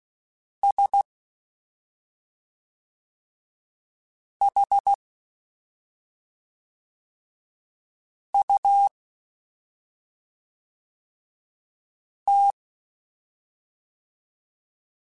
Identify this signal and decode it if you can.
Morse code, heard as a keyed tone, about 16 wpm, 800 Hz. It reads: S H U T
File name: Special INGs / Special INGs-mp3s / SlowToFaster ing @5